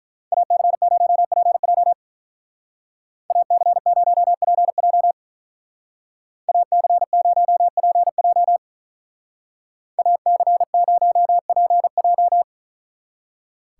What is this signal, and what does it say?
Morse code, heard as a keyed tone, about 43 wpm, 700 Hz. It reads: AC0PJ AC0PJ AC0PJ AC0PJ
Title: Speed Racing - USA Call Signs 35wpm